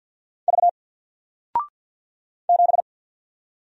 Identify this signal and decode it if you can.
Morse code, heard as a keyed tone, about 50 wpm, 700 Hz. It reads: V 7
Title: Single Character (Letter:Number:Punctuation:Prosign) 50wpm